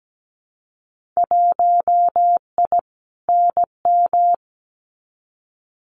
Morse code, heard as a keyed tone, about 17 wpm, 700 Hz. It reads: 1I NM
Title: Winter Field Day Exchanges Encore 17wpm